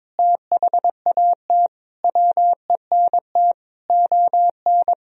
Morse code, heard as a keyed tone, about 22 wpm, 700 Hz. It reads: THAT WENT ON
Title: Sentences from Top 1000 Words - Review Part 3 - 22wpm